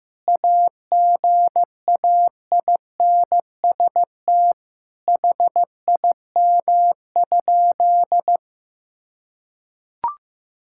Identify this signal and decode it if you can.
Morse code, heard as a keyed tone, about 15 wpm, 700 Hz. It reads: AGAINST HIM?
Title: Sentences from Top 1000 Words - Review Part 6 - 15wpm